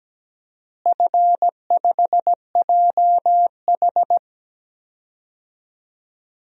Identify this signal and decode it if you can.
Morse code, heard as a keyed tone, about 17 wpm, 700 Hz. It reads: F5JH
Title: DXCC-100 Call Signs - 4 Characters - Repeated 2x - 17wpm